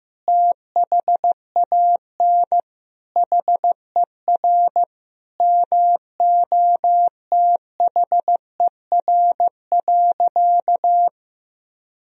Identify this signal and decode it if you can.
Morse code, heard as a keyed tone, about 16 wpm, 700 Hz. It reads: THAN HER MOTHER.
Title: Sentences from Top 400 Words and Abbreviations 15wpm